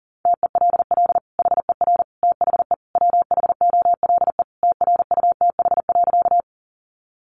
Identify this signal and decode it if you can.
Morse code, heard as a keyed tone, about 40 wpm, 700 Hz. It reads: TELL HER THE WHOLE TRUTH.
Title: Sentences from Top 800 Words 40wpm